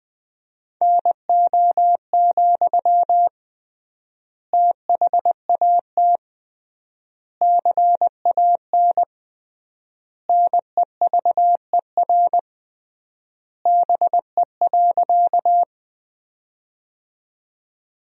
Morse code, heard as a keyed tone, about 20 wpm, 700 Hz. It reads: NO, THAT CAN NEVER BE.
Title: 3x Word-Spacing - Sentences from Top 300 Words - 20wpm